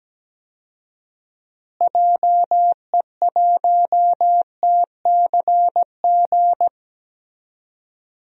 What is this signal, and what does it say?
Morse code, heard as a keyed tone, about 17 wpm, 700 Hz. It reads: JE1TCG
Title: DXCC-100 Call Signs - 6 Characters - Repeated 3x - 17wpm